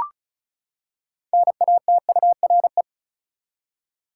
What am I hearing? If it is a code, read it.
Morse code, heard as a keyed tone, about 36 wpm, 700 Hz. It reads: NATURE
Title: Six Letter Words - 35wpm